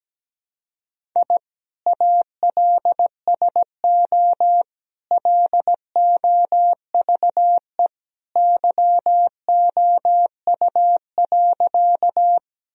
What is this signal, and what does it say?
Morse code, heard as a keyed tone, about 17 wpm, 700 Hz. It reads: I ALSO LOVE YOU.
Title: Sentences from Top 1000 Words - Review Part 1 - 17wpm